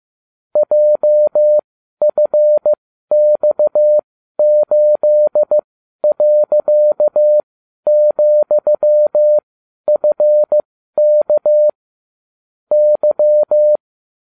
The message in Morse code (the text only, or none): JFX8.,FK Y